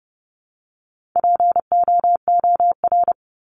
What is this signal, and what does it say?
Morse code, heard as a keyed tone, about 30 wpm, 700 Hz. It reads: POOR
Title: Top 751-1000 Words 30wpm